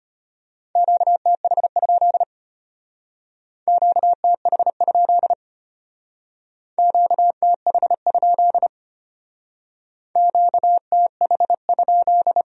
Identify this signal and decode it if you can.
Morse code, heard as a keyed tone, about 32 wpm, 700 Hz. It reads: QTH? QTH? QTH? QTH?